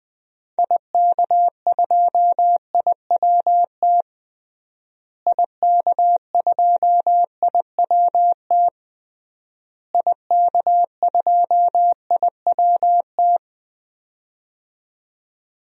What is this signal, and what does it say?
Morse code, heard as a keyed tone, about 21 wpm, 700 Hz. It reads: IK2IWT IK2IWT IK2IWT
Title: DXCC-100 Call Signs - 6 Characters - Repeated 3x - 20wpm